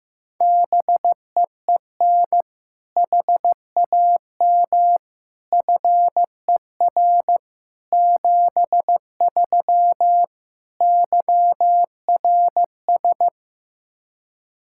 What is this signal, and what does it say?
Morse code, heard as a keyed tone, about 15 wpm, 700 Hz. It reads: BEEN HAM FER 73 YRS
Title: K1USN SST QSO Practice 15wpm